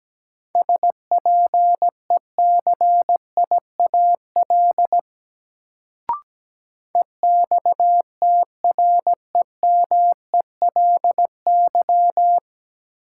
Morse code, heard as a keyed tone, about 17 wpm, 700 Hz. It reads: SPECIAL EXTREMELY
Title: Sentences from Top 1200 Words - New Word Drill Alt - 17wpm